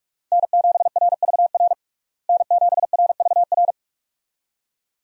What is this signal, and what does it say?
Morse code, heard as a keyed tone, about 46 wpm, 700 Hz. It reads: N7RVR N7RVR
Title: USA Call Signs - 5 Characters - Repeated 2x - 45wpm